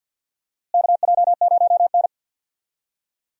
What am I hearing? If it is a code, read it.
Morse code, heard as a keyed tone, about 52 wpm, 700 Hz. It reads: KJ0N
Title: USA Call Signs - 4 Characters - Repeated 3x - 50wpm